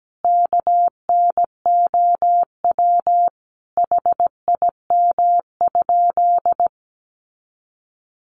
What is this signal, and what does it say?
Morse code, heard as a keyed tone, about 17 wpm, 700 Hz. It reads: KNOW HIM?
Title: Sentences from Top 100 Words 17wpm